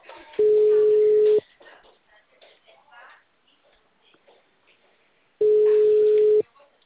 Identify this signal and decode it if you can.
Ringback tone (Europe, 425 Hz), with speech.
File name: out- -2031- - - .5653